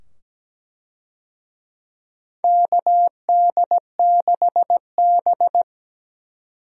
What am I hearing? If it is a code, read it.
Morse code, heard as a keyed tone, about 17 wpm, 700 Hz. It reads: KD6B
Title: USA Call Signs - 4 Characters - 17wpm